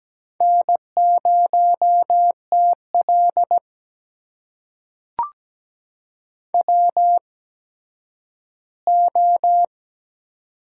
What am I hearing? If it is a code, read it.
Morse code, heard as a keyed tone, about 17 wpm, spaced as for 10 wpm, 700 Hz. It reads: N0TL WO
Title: USA Call Signs - 4 Characters - 8x Character Spacing - 17wpm